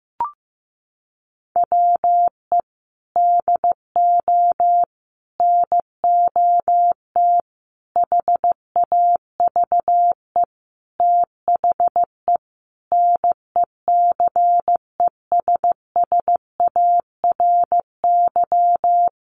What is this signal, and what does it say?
Morse code, heard as a keyed tone, about 14 wpm, 700 Hz. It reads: WE DO NOT HAVE THE NECESSARY
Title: Sentences from Top 1200 Words 15wpm